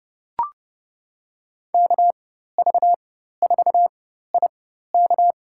Morse code, heard as a keyed tone, about 30 wpm, spaced as for 14 wpm, 700 Hz. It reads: KV4IK
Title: USA Call Signs - 5 Characters - 4x Character Spacing - 30wpm